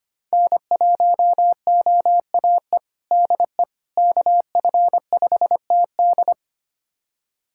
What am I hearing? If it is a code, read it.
Morse code, heard as a keyed tone, about 25 wpm, 700 Hz. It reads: N1OAE DE KF5TD
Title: USA Callsign DE Callsign - 5 Characters - 25wpm